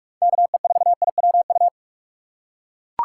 Morse code, heard as a keyed tone, about 41 wpm, 700 Hz. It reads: KE4IWU
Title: USA Call Signs - 6 Characters - 4x Character Spacing - 45wpm